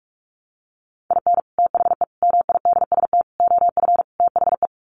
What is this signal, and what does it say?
Morse code, heard as a keyed tone, about 44 wpm, 700 Hz. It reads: IN THE MIDST OF THE
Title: Sets of 5 Words 45wpm